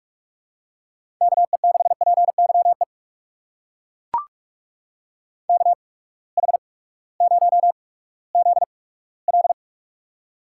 Morse code, heard as a keyed tone, about 42 wpm, spaced as for 20 wpm, 700 Hz. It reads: KE6PYE KH0ZL
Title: USA Call Signs - 8x Character Spacing - 45wpm